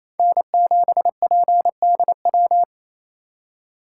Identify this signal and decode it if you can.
Morse code, heard as a keyed tone, about 28 wpm, 700 Hz. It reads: N7PDW
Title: USA Call Signs - 5 Characters - 4x Character Spacing - 28wpm